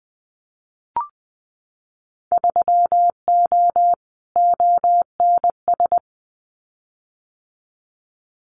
Morse code, heard as a keyed tone, about 20 wpm, 700 Hz. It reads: 3O ONS
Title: Winter Field Day Exchanges Encore 20wpm